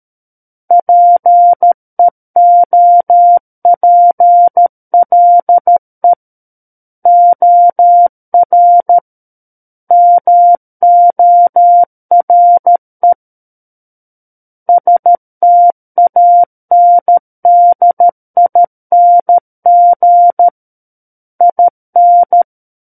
Morse code, heard as a keyed tone, about 13 wpm, 700 Hz. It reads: PEOPLE OR MORE STANDING IN
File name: Book One / 04-Chapter 4 - 13 WPM